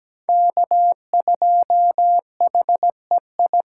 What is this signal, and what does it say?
Morse code, heard as a keyed tone, about 17 wpm, 700 Hz. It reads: K2HEI
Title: USA Callsign DE Callsign - 5 Characters - 17wpm